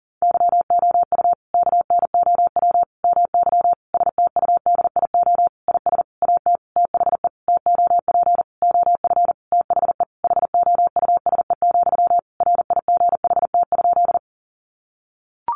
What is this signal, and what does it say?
Morse code, heard as a keyed tone, about 40 wpm, 700 Hz. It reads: YOU KNOW MY STUDIO IS AT THE TOP OF THE HOUSE, RIGHT?
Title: Sentences from Top 1200 Words 40wpm